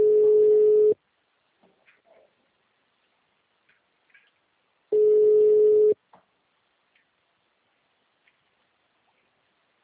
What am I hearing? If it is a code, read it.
Ringback tone (Europe, 425 Hz).